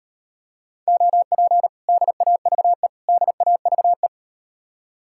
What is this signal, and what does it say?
Morse code, heard as a keyed tone, about 38 wpm, 700 Hz. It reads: OP DAVE DAVE